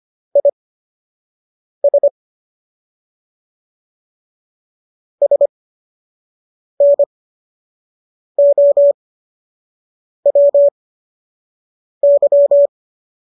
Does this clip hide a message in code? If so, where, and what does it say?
Yes: Morse code from 0.4 to 12.7 s, heard as a keyed tone, about 25 wpm, spaced as for 6 wpm, 570 Hz. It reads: IS SNOWY